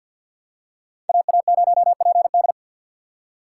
Morse code, heard as a keyed tone, about 49 wpm, 700 Hz. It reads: AA0PD
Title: USA Call Signs - 5 Characters - 4x Character Spacing - 50wpm